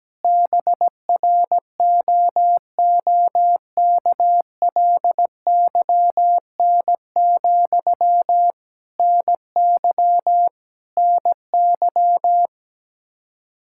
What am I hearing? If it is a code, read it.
Morse code, heard as a keyed tone, about 17 wpm, 700 Hz. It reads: BROOKLYN, NY NY
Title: Practice QSO - Exchange 1 - Beginning - 17wpm